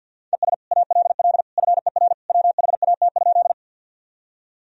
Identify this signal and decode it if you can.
Morse code, heard as a keyed tone, about 52 wpm, 700 Hz. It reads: ES ALL FER WHAT?
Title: Sentences from Top 100 Words and Abbreviations 50wpm